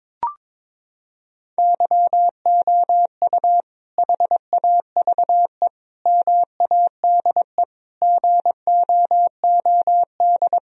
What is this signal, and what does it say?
Morse code, heard as a keyed tone, about 22 wpm, 700 Hz. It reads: YOU HAVE MADE GOOD
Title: Sentences from Top 1000 Words - Review Part 1 - 22wpm